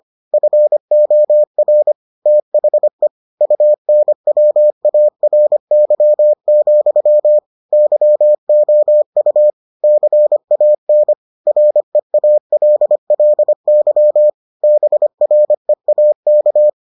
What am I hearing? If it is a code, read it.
Morse code, heard as a keyed tone, about 25 wpm, 600 Hz. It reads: FOR THE UNWARY, YOU CAN REALLY BREAK